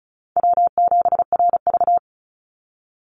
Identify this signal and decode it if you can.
Morse code, heard as a keyed tone, about 34 wpm, 700 Hz. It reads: W7RV
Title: NA SOTA Chaser Callsigns 35wpm